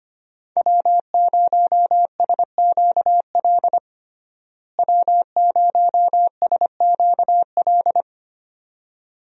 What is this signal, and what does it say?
Morse code, heard as a keyed tone, about 25 wpm, 700 Hz. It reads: W0SQL W0SQL
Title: USA Call Signs - 5 Characters - Repeated 2x - 25wpm